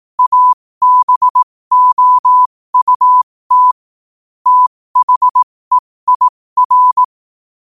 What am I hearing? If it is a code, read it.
Morse code, heard as a keyed tone, about 18 wpm, spaced as for 15 wpm, 1000 Hz. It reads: ABOUT THEIR
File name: qotdfiles / 15wpm